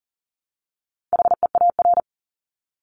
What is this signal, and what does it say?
Morse code, heard as a keyed tone, about 40 wpm, 700 Hz. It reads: HEAR
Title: Top 300 Words Repeated 40wpm